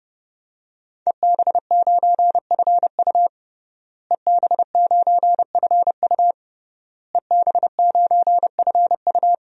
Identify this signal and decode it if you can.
Morse code, heard as a keyed tone, about 32 wpm, 700 Hz. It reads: EB9FU EB9FU EB9FU
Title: DXCC-100 Call Signs - 5 Characters - Repeated 3x - 30wpm